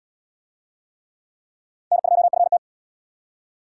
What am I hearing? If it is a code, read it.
Morse code, heard as a keyed tone, about 65 wpm, 700 Hz. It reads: N2FT